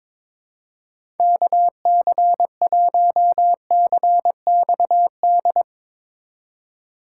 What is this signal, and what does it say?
Morse code, heard as a keyed tone, about 22 wpm, 700 Hz. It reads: KC1CXD
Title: USA Call Signs - 6 Characters - 2x Character Spacing - 22wpm